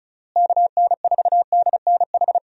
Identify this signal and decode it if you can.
Morse code, heard as a keyed tone, about 34 wpm, 700 Hz. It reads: KN4DNH
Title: USA Call Signs - 6 Characters - Repeated 3x - 35wpm